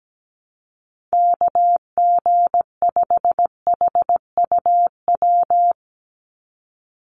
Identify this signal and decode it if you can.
Morse code, heard as a keyed tone, about 17 wpm, 700 Hz. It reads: KG5HUW
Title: USA Call Signs 17wpm